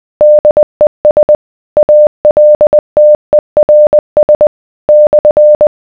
Morse code, heard as a keyed tone, about 20 wpm, 600 Hz. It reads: DES ALTERS /